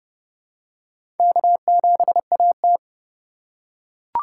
Morse code, heard as a keyed tone, about 31 wpm, 700 Hz. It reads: K7AT
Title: USA Call Signs - 4 Characters - 4x Character Spacing - 30wpm